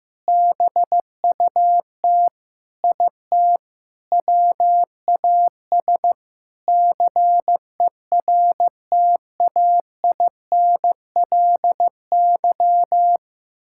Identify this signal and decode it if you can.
Morse code, heard as a keyed tone, about 15 wpm, 700 Hz. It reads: BUT IT WAS CERTAINLY
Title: Sentences from Top 1000 Words - Review Part 6 - 15wpm